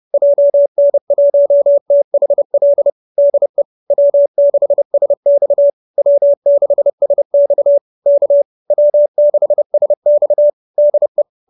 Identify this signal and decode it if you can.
Morse code, heard as a keyed tone, about 30 wpm, 590 Hz. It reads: JN1THL DE W6SX W6SX K W6SX DE